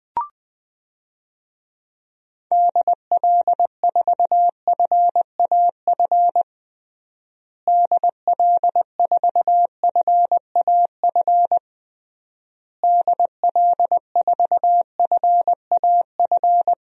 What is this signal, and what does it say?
Morse code, heard as a keyed tone, about 21 wpm, 700 Hz. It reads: DL4FAF DL4FAF DL4FAF
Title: DXCC-100 Call Signs - 6 Characters - Repeated 3x - 20wpm